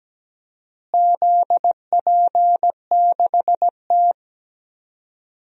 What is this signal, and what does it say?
Morse code, heard as a keyed tone, about 17 wpm, 700 Hz. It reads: ZP6T
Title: DXCC-100 Call Signs - 4 Characters - Repeated 2x - 17wpm